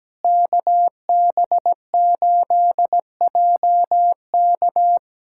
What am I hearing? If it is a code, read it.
Morse code, heard as a keyed tone, about 17 wpm, 700 Hz. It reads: KB8JK
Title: USA Callsign DE Callsign - 5 Characters - 17wpm